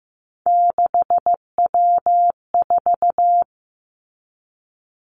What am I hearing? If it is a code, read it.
Morse code, heard as a keyed tone, about 15 wpm, 700 Hz. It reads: 6W4
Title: Three Challenging Letter-Number Combinations 15wpm